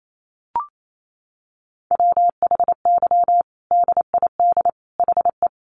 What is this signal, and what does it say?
Morse code, heard as a keyed tone, about 28 wpm, 700 Hz. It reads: WHY DID HE
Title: Sentences from Top 1100 Words 28wpm